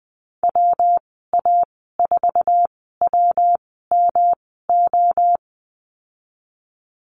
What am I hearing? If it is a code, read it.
Morse code, heard as a keyed tone, about 20 wpm, spaced as for 14 wpm, 700 Hz. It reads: WA4WMO
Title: 154 Lesson 25g - Letter p - Call Signs 20wpm